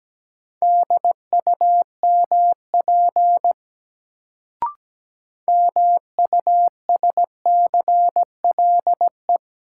Morse code, heard as a keyed tone, about 17 wpm, 700 Hz. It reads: DUMP MUSCLE
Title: Sentences from Top 1300 Words - New Word Drill - 17wpm